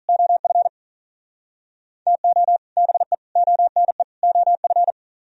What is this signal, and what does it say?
Morse code, heard as a keyed tone, about 48 wpm, 700 Hz. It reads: OF TO BE ONE OF